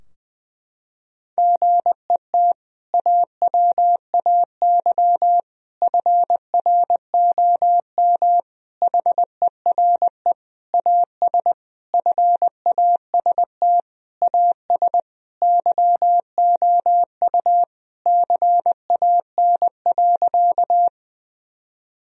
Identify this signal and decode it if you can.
Morse code, heard as a keyed tone, about 20 wpm, 700 Hz. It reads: GET AWAY FROM HERE AS FAST AS YOU CAN.
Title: Sentences from Top 1000 Words - Review Part 2 - 20wpm